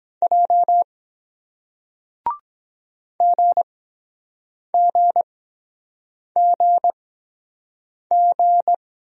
Morse code, heard as a keyed tone, about 23 wpm, spaced as for 7 wpm, 700 Hz. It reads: J GGGG